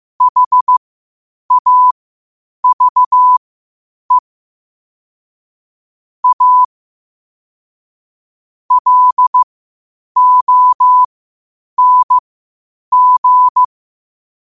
Morse code, heard as a keyed tone, about 15 wpm, spaced as for 8 wpm, 1000 Hz. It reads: HAVE A LONG
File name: qotdfiles / 8wpm